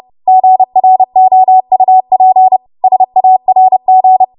Morse code, heard as a keyed tone, about 30 wpm, 750 Hz. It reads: GROUP SARG
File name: Morse / 30WPM